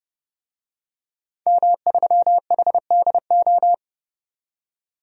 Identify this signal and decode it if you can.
Morse code, heard as a keyed tone, about 30 wpm, 700 Hz. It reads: M3HDO
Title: DXCC-100 Call Signs - 5 Characters - 30wpm